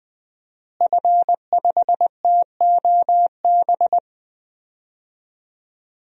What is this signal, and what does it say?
Morse code, heard as a keyed tone, about 20 wpm, 700 Hz. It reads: F5TOB
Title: DXCC-100 Call Signs - 5 Characters - Repeated 3x - 20wpm